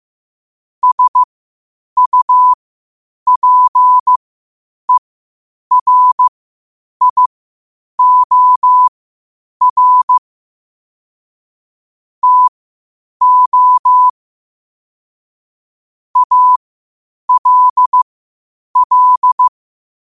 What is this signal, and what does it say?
Morse code, heard as a keyed tone, about 15 wpm, spaced as for 8 wpm, 1000 Hz. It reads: SUPERIOR TO ALL